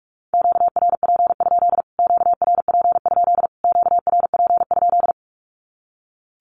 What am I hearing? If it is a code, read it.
Morse code, heard as a keyed tone, about 44 wpm, 700 Hz. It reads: QRP? QRP? QRP?
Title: Q-Codes Repeated 3x 45wpm